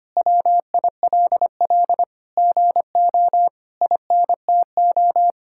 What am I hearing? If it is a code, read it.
Morse code, heard as a keyed tone, about 25 wpm, 700 Hz. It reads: WILL GO INTO
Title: Sentences from Top 1000 Words - Review Part 6 - 25wpm